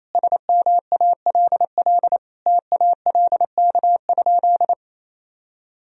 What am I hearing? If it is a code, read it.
Morse code, heard as a keyed tone, about 28 wpm, 700 Hz. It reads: SMALL TALK?
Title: Sentences from Top 1000 Words - Review Part 1 - 28wpm